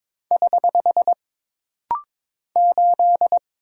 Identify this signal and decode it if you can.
Morse code, heard as a keyed tone, about 22 wpm, 700 Hz. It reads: <HH> 8
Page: This rapid-fire timing is geared to push your ICR (Instant Character Recognition) skills.